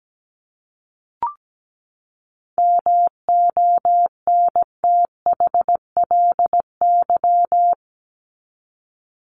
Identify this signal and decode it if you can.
Morse code, heard as a keyed tone, about 17 wpm, 700 Hz. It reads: MONTHLY
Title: LY Suffix - Top 100 Words with Base Words 17wpm